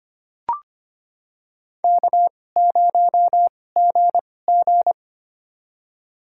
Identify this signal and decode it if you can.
Morse code, heard as a keyed tone, about 25 wpm, spaced as for 18 wpm, 700 Hz. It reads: K0GG
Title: USA Call Signs - 4 Characters - 2x Character Spacing - 25wpm